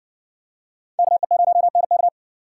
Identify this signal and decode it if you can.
Morse code, heard as a keyed tone, about 60 wpm, 700 Hz. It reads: KE0NK